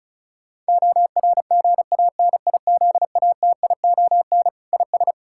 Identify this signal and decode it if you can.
Morse code, heard as a keyed tone, about 36 wpm, 700 Hz. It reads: ORGANIZATION IS
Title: Sentences from Top 1000 Words - Review Part 5 - 35wpm